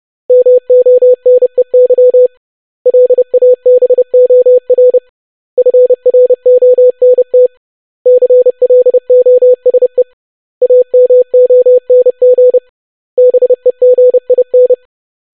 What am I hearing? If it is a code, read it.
Morse code, heard as a keyed tone, about 30 wpm, 500 Hz. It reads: MONEY LABOR FRONT CLOSE AMONG BEGIN
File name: basic500-25wpm